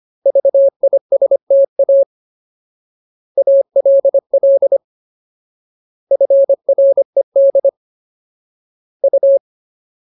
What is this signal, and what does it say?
Morse code, heard as a keyed tone, about 25 wpm, 550 Hz. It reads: VISTA ALL FRED U
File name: Ultimate Word List / 25WPM Ultimate